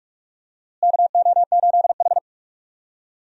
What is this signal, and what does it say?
Morse code, heard as a keyed tone, about 44 wpm, 700 Hz. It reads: KO8H
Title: USA Call Signs - 4 Characters - Repeated 2x - 45wpm